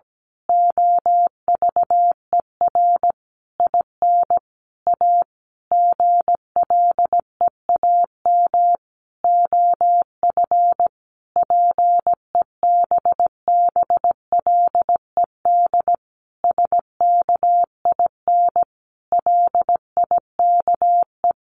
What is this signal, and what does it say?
Morse code, heard as a keyed tone, about 17 wpm, 700 Hz. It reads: OVER IN A GLEAM OF PEBBLED SKIN LIKE
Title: A Sound Of Thunder 17wpm